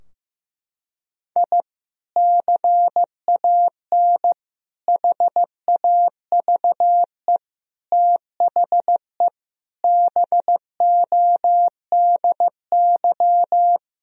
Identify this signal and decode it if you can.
Morse code, heard as a keyed tone, about 15 wpm, 700 Hz. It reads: I CAN HAVE THE BODY
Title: Sentences from Top 1000 Words - Review Part 2 - 15wpm